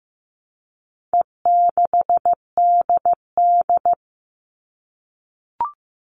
Morse code, heard as a keyed tone, about 15 wpm, 700 Hz. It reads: E6DD
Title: All Cut Numbers - 4-Digit 15wpm